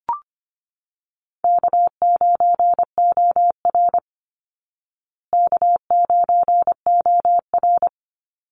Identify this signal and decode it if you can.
Morse code, heard as a keyed tone, about 25 wpm, 700 Hz. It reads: K9OR K9OR
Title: Easy Call Signs Repeated 2x - 25wpm